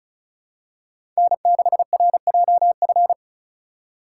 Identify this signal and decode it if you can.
Morse code, heard as a keyed tone, about 34 wpm, 700 Hz. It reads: N6RJF
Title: USA Call Signs - 5 Characters - 3x Character Spacing - 35wpm